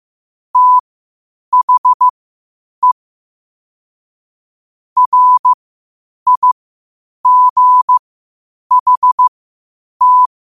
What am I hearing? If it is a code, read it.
Morse code, heard as a keyed tone, about 15 wpm, spaced as for 8 wpm, 1000 Hz. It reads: THE RIGHT
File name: qotdfiles / 8wpm